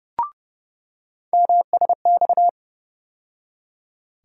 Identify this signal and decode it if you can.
Morse code, heard as a keyed tone, about 30 wpm, 700 Hz. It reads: MSX
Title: USA Call Sign Postfixes - 3 Letters 30wpm